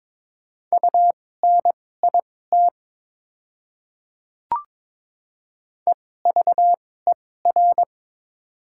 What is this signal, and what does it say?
Morse code, heard as a keyed tone, about 22 wpm, spaced as for 16 wpm, 700 Hz. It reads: UNIT EVER
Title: 2x Character Spacing - Top 500 Words - 22wpm